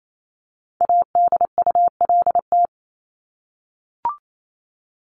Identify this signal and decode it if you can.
Morse code, heard as a keyed tone, about 28 wpm, 700 Hz. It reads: ADULT
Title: Top 1501-1750 Words 28wpm